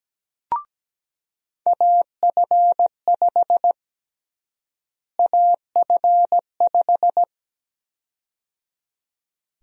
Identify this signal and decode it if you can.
Morse code, heard as a keyed tone, about 17 wpm, 700 Hz. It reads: AF5 AF5
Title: USA Call Sign Prefixes - Repeated 2x - 17wpm